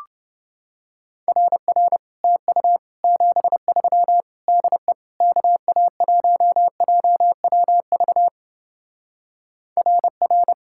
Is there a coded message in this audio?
Yes: Morse code from 1.3 to 10.5 s, heard as a keyed tone, about 28 wpm, 700 Hz. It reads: RR TU 73 DE KA1JWV RR